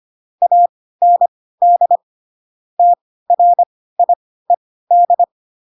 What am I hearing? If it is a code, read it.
Morse code, heard as a keyed tone, about 25 wpm, spaced as for 16 wpm, 700 Hz. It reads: AND TRIED
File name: Pinocchio / Pinocchio ch3 10 25 ik1hge